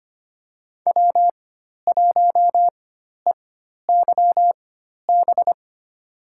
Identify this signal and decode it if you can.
Morse code, heard as a keyed tone, about 25 wpm, spaced as for 12 wpm, 700 Hz. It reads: W1EYB
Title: USA Call Signs - 5 Characters - 4x Character Spacing - 25wpm